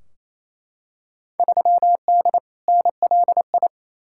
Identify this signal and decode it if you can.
Morse code, heard as a keyed tone, about 28 wpm, 700 Hz. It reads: 3D NLI
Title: Field Day Exchanges Encore - 28wpm